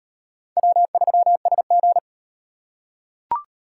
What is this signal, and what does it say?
Morse code, heard as a keyed tone, about 35 wpm, 700 Hz. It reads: W3SG